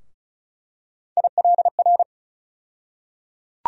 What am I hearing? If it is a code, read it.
Morse code, heard as a keyed tone, about 34 wpm, 700 Hz. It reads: ILR
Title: USA Call Sign Postfixes - 3 Letters 35wpm